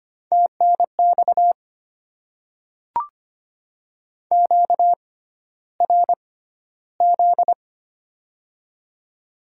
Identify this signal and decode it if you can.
Morse code, heard as a keyed tone, about 25 wpm, spaced as for 13 wpm, 700 Hz. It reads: TNX QRZ
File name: Top 90 QSO Elements - 6x Character Spacing - 25wpm